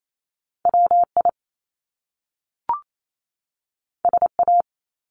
Morse code, heard as a keyed tone, about 28 wpm, 700 Hz. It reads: WI SA
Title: Top Two Letter Combinations in Words 28wpm